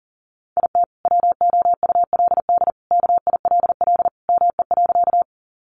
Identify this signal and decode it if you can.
Morse code, heard as a keyed tone, about 40 wpm, 700 Hz. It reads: IT WOULD KILL ME.
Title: Sentences from Top 1400 Words 40wpm